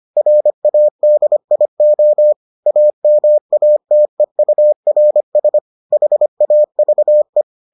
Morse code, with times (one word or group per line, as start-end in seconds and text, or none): 0.2-2.3 RADIO
2.7-5.6 AMATEURS
5.9-7.4 HAVE